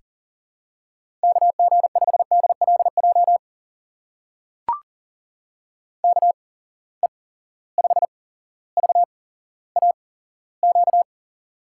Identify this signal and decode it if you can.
Morse code, heard as a keyed tone, about 40 wpm, spaced as for 17 wpm, 700 Hz. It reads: KG5DLJ KE5VAQ
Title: USA Call Signs - 6 Characters - 8x Character Spacing - 40wpm